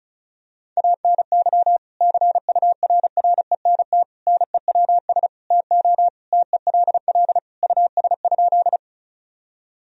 Morse code, heard as a keyed tone, about 36 wpm, 700 Hz. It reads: ANY CURRENT NEWS TO TELL US?
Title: Sentences from Top 1000 Words - Review Part 4 - 35wpm